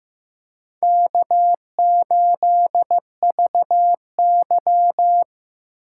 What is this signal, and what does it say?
Morse code, heard as a keyed tone, about 15 wpm, 700 Hz. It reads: K8VY
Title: USA Call Signs - 4 Characters - 2x Character Spacing - 15wpm